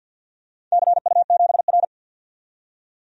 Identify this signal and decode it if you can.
Morse code, heard as a keyed tone, about 52 wpm, 700 Hz. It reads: CU7R
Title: DXCC-100 Call Signs - 4 Characters - 50wpm